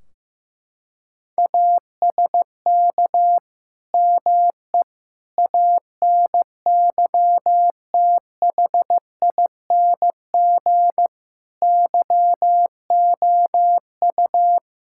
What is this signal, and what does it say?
Morse code, heard as a keyed tone, about 15 wpm, 700 Hz. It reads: ASK ME ANYTHING YOU
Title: Sentences from Top 1600 Words 15wpm